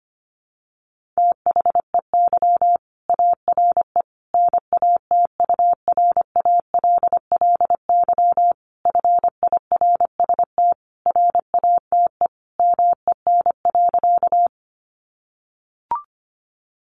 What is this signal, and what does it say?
Morse code, heard as a keyed tone, about 25 wpm, 700 Hz. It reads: THEY ARE NATURALLY FIRST RATE MEN.
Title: Sentences from Top 1300 Words 25wpm